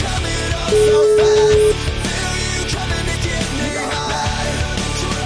Ringback tone (Europe, 425 Hz), with noise.